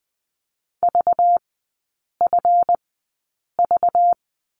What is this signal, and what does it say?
Morse code, heard as a keyed tone, about 20 wpm, spaced as for 8 wpm, 690 Hz. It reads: VFV
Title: 191 Lesson 31e - Letter v - Three Characters 20wpm